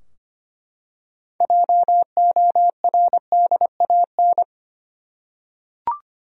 Morse code, heard as a keyed tone, about 25 wpm, 700 Hz. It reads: JORDAN
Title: USA Common Names - 25wpm